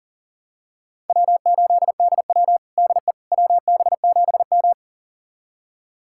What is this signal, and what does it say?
Morse code, heard as a keyed tone, about 40 wpm, 700 Hz. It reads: W8DW DE WB7M
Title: USA Callsign DE Callsign - 4 Characters - 40wpm